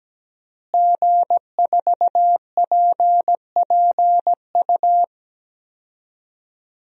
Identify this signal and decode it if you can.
Morse code, heard as a keyed tone, about 17 wpm, 700 Hz. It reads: G4PPU
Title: DXCC-100 Call Signs - 5 Characters - Repeated 2x - 17wpm